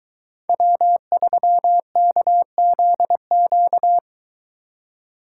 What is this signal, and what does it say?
Morse code, heard as a keyed tone, about 23 wpm, 700 Hz. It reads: W3KZQ